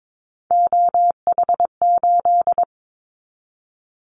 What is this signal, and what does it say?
Morse code, heard as a keyed tone, about 22 wpm, 700 Hz. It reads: OH8
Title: Callsign Prefixes for Top 25 DXCC - 22wpm